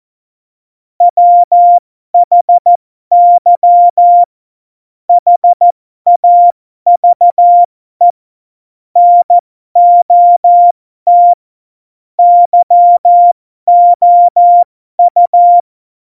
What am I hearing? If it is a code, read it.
Morse code, heard as a keyed tone, about 14 wpm, spaced as for 12 wpm, 700 Hz. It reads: WHY HAVE NOT YOU
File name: Morsekurs / Witze